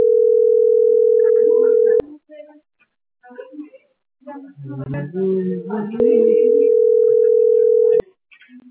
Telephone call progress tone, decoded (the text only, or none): ringback tone